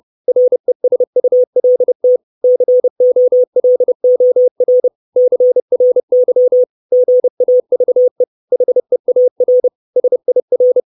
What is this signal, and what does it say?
Morse code, heard as a keyed tone, about 30 wpm, 500 Hz. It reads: RESULT COLOR CRY GAVE HEAR SIR
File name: 632 Words 30WPM